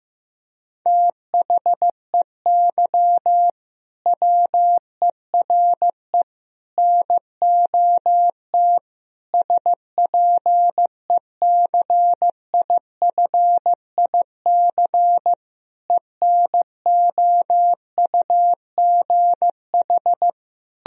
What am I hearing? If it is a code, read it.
Morse code, heard as a keyed tone, about 15 wpm, 700 Hz. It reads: THEY WERE NOT SPECIFIC ENOUGH
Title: Sentences from Top 1000 Words - Review Part 3 - 15wpm